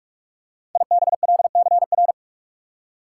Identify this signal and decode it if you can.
Morse code, heard as a keyed tone, about 46 wpm, 700 Hz. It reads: IBLCR
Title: Code Groups - 5 Letter:Numbers - 45wpm